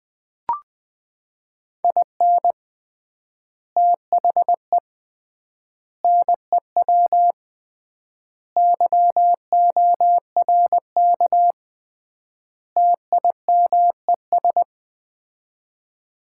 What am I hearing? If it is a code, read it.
Morse code, heard as a keyed tone, about 20 wpm, 700 Hz. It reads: IN THE NEW YORK TIMES
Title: 3x Word-Spacing - Sets of 5 Words - 20wpm